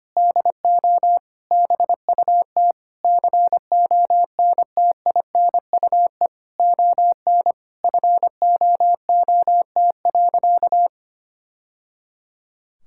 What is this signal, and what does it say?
Morse code, heard as a keyed tone, about 25 wpm, 700 Hz. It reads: DO BUT CONTINUE ON FOOT.
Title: Sentences from Top 1000 Words - Review Part 6 - 25wpm